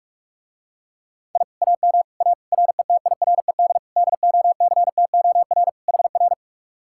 Morse code, heard as a keyed tone, about 49 wpm, 700 Hz. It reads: I AM A RETIRED DOCTOR HR